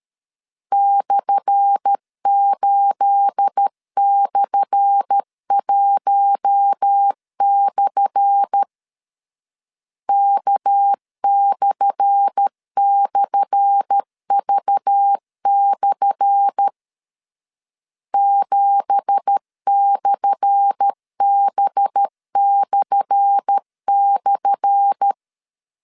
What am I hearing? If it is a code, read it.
Morse code, heard as a keyed tone, about 13 wpm, 800 Hz. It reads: /8/1/ K//V/ 7/B//
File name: LEZIONI / 10 SIGNS / 10A SLASH 12wpm 2314